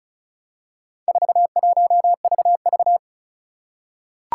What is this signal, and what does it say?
Morse code, heard as a keyed tone, about 36 wpm, 700 Hz. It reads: 41VV
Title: Four Challenging Letter-Number Combinations 35wpm